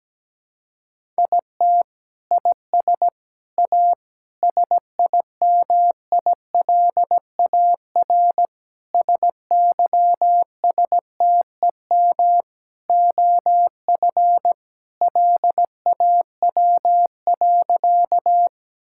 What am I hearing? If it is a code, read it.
Morse code, heard as a keyed tone, about 17 wpm, 700 Hz. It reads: IT IS A SIMILAR SYSTEM OF LAW.
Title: Sentences from Top 1000 Words - Review Part 5 - 17wpm